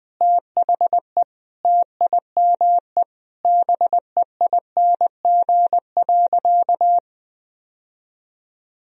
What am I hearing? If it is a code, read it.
Morse code, heard as a keyed tone, about 21 wpm, 700 Hz. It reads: THE TIME BEING.
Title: Sentences from Top 1000 Words - Review Part 4 - 20wpm